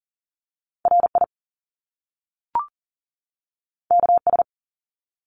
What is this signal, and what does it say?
Morse code, heard as a keyed tone, about 38 wpm, 700 Hz. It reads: RI KS
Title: US States 40wpm